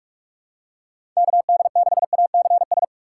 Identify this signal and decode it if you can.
Morse code, heard as a keyed tone, about 42 wpm, 700 Hz. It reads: KD6ACS
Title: USA Call Signs - 6 Characters - 8x Character Spacing - 45wpm